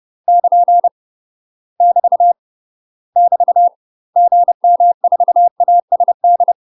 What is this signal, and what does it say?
Morse code, heard as a keyed tone, about 30 wpm, 700 Hz. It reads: <KN> <BT> <BT> GM4ASD
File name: UK MASTER QSOs TRAINING / 30WPM / UK MASTER QSOs 0020